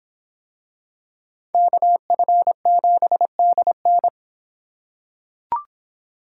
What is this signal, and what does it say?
Morse code, heard as a keyed tone, about 24 wpm, 700 Hz. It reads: KF7DN